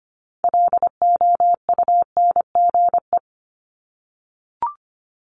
Morse code, heard as a keyed tone, about 25 wpm, 700 Hz. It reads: LOUNGE
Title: Multi-letter Phonemes 25wpm